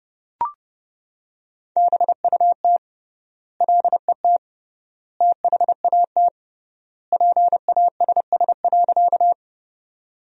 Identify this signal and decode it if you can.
Morse code, heard as a keyed tone, about 30 wpm, 700 Hz. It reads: BUT LET THAT PASS.
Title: 3x Word-Spacing - Sentences from Top 400 Words - 30wpm